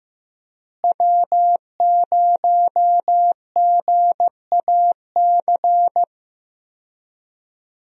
Morse code, heard as a keyed tone, about 15 wpm, 700 Hz. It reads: W0GAC
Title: USA Call Signs - 5 Characters - Repeated 2x - 15wpm